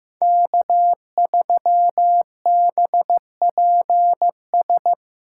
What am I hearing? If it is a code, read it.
Morse code, heard as a keyed tone, about 17 wpm, 700 Hz. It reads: K3BPS
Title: Speed Racing - USA 1x3 Call Signs 15wpm